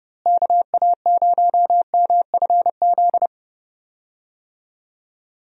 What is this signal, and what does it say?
Morse code, heard as a keyed tone, about 30 wpm, 700 Hz. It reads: KA0MFZ
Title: USA Call Signs - 6 Characters - Repeated 3x - 30wpm